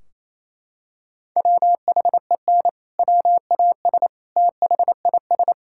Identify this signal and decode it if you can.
Morse code, heard as a keyed tone, about 28 wpm, 700 Hz. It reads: WHEN WAS THIS
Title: Sentences from Top 1600 Words 28wpm